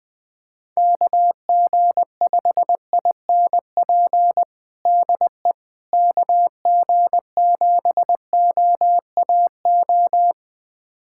Morse code, heard as a keyed tone, about 20 wpm, 700 Hz. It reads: KG5INP DE KG7OAO
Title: USA Callsign DE Callsign - 6 Characters - 20wpm